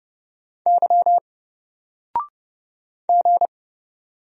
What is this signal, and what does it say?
Morse code, heard as a keyed tone, about 31 wpm, 700 Hz. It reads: Y G
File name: Single-Letter-Number-30wpm